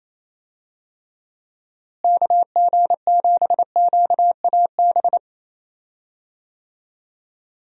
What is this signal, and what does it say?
Morse code, heard as a keyed tone, about 28 wpm, 700 Hz. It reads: KG7QAB
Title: USA Call Signs - 6 Characters - Repeated 3x - 28wpm